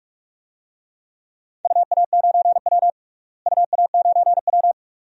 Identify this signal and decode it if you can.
Morse code, heard as a keyed tone, about 46 wpm, 700 Hz. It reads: UA9W UA9W
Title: DXCC-100 Call Signs - 4 Characters - Repeated 2x - 45wpm